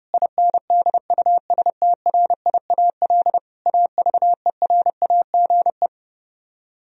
Morse code, heard as a keyed tone, about 31 wpm, 700 Hz. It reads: INDUSTRIAL AVERAGE